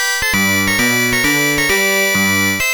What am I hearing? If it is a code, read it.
There is no code.